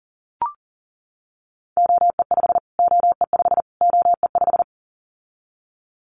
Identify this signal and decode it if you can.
Morse code, heard as a keyed tone, about 40 wpm, 700 Hz. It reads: OE5 OE5 OE5
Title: Callsign Prefixes for Top 25 DXCC - Repeated 3x - 40wpm